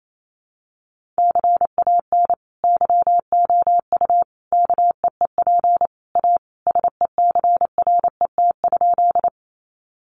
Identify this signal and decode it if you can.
Morse code, heard as a keyed tone, about 28 wpm, 700 Hz. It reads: CAN YOU KEEP A SECRET?
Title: Sentences from Top 1200 Words 28wpm